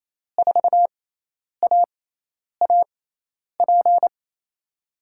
Morse code, heard as a keyed tone, about 28 wpm, spaced as for 10 wpm, 700 Hz. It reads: 4AAP
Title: USA Call Signs - 5 Characters - 6x Character Spacing - 28wpm